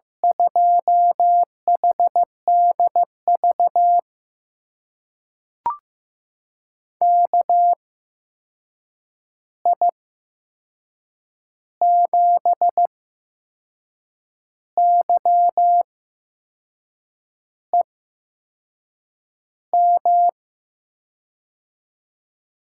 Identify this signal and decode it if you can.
Morse code, heard as a keyed tone, about 15 wpm, spaced as for 6 wpm, 700 Hz. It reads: 2HDV KI7YEM
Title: USA Call Signs - 8x Character Spacing - 15wpm